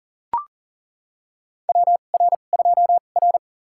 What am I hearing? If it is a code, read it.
Morse code, heard as a keyed tone, about 39 wpm, spaced as for 28 wpm, 690 Hz. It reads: WR2R
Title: USA Call Signs - 4 Characters - 2x Character Spacing - 40wpm